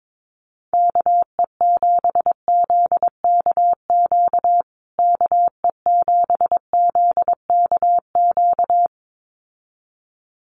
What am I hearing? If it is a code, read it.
Morse code, heard as a keyed tone, about 22 wpm, 700 Hz. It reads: KE7ZKQ KE7ZKQ
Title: USA Call Signs Repeated 2x 22wpm